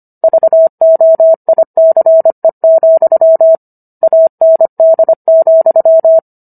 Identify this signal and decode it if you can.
Morse code, heard as a keyed tone, about 25 wpm, 650 Hz. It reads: VOICE, AND,